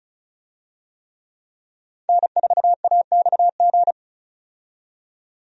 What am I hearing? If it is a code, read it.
Morse code, heard as a keyed tone, about 34 wpm, 700 Hz. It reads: N4AXG
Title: USA Call Signs - 5 Characters - 35wpm